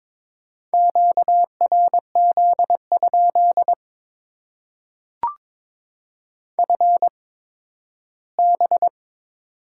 Morse code, heard as a keyed tone, about 22 wpm, spaced as for 13 wpm, 690 Hz. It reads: QRZ? FB
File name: Top 90 QSO Elements - 8x Character Spacing - 22wpm